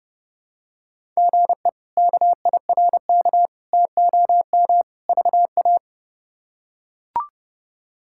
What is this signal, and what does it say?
Morse code, heard as a keyed tone, about 32 wpm, 700 Hz. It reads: GE KIRK TOM VA
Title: K1USN SST QSO Practice - 30wpm